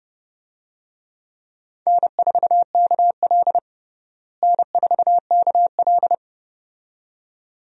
Morse code, heard as a keyed tone, about 31 wpm, 700 Hz. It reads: N4KL N4KL
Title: USA Call Signs - 4 Characters - Repeated 2x - 30wpm